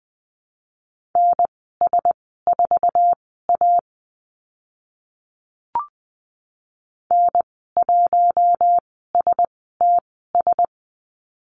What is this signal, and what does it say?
Morse code, heard as a keyed tone, about 20 wpm, spaced as for 14 wpm, 700 Hz. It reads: NS4A N1STS
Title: 055 Lesson 10g - Letter r - Call Signs 20wpm